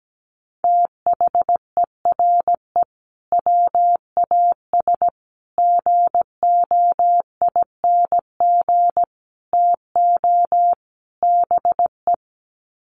Morse code, heard as a keyed tone, about 17 wpm, 700 Hz. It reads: THERE WAS GOING TO BE
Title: Sets of 5 Words Encore2 17wpm